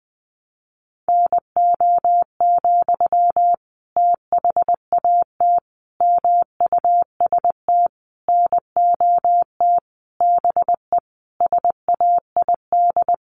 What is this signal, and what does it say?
Morse code, heard as a keyed tone, about 19 wpm, 700 Hz. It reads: NO, THAT MUST NOT BE SAID
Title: Sentences from Top 600 Words 20wpm